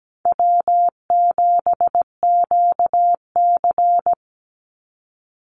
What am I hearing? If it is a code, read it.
Morse code, heard as a keyed tone, about 17 wpm, 700 Hz. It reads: W7QC
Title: Call Signs Repeated 2x - 17wpm